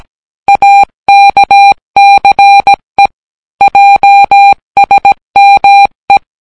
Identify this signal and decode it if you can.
Morse code, heard as a keyed tone, about 17 wpm, 800 Hz. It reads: AKCE JSME